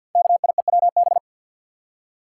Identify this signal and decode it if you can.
Morse code, heard as a keyed tone, about 46 wpm, 700 Hz. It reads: KIEWB
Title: Code Groups - 5 Letters - 50wpm